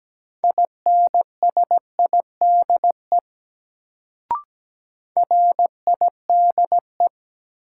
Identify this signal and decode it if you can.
Morse code, heard as a keyed tone, about 17 wpm, 700 Hz. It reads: INSIDE RIDE
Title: Sentences from Top 1200 Words - New Word Drill Alt - 17wpm